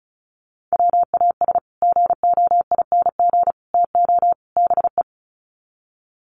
Morse code, heard as a keyed tone, about 34 wpm, 700 Hz. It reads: WAS GOING TO BE
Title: Sets of 4 Words 35wpm